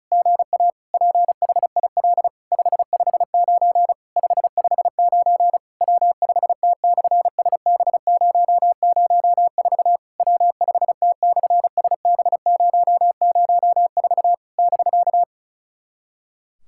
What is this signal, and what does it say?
Morse code, heard as a keyed tone, about 36 wpm, 690 Hz. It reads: GA PHIL 559 559 W5T/SB004 W5T/SB004 <BK>
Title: SOTA QSO Practice 35wpm